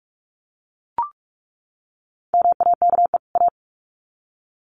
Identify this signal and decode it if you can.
Morse code, heard as a keyed tone, about 44 wpm, 700 Hz. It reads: MAKE A
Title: Sets of 2 Words 45wpm